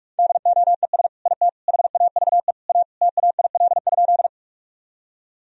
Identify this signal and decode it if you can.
Morse code, heard as a keyed tone, about 46 wpm, 700 Hz. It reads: DOES IT HAVE A TAIL?
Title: Sentences from Top 1000 Words - Review Part 3 - 45wpm